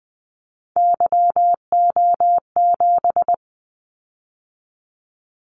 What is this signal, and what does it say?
Morse code, heard as a keyed tone, about 20 wpm, 700 Hz. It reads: YO7
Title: Callsign Prefixes for Top 25 DXCC - 20wpm